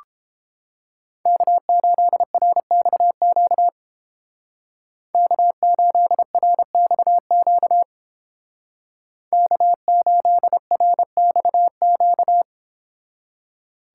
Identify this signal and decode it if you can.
Morse code, heard as a keyed tone, about 30 wpm, 700 Hz. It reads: K8RXQ K8RXQ K8RXQ